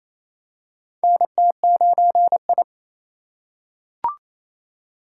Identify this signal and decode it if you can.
Morse code, heard as a keyed tone, about 28 wpm, 700 Hz. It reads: NT9I
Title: USA Call Signs - 2x Character Spacing - 28wpm